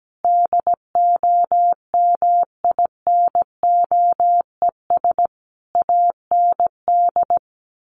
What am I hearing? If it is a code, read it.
Morse code, heard as a keyed tone, about 17 wpm, 700 Hz. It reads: DOMINOES AND
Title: A Sound Of Thunder 17wpm